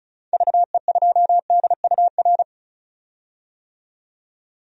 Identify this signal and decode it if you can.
Morse code, heard as a keyed tone, about 36 wpm, 700 Hz. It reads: VE2DUR
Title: DXCC-100 Call Signs - 6 Characters - 35wpm